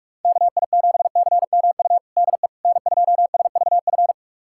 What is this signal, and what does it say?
Morse code, heard as a keyed tone, about 46 wpm, 700 Hz. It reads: KI7CMU DE N2SVF
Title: USA Callsign DE Callsign 45wpm